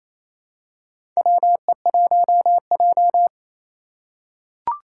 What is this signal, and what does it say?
Morse code, heard as a keyed tone, about 28 wpm, 700 Hz. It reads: WE1J
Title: USA Call Signs - 4 Characters - 3x Character Spacing - 28wpm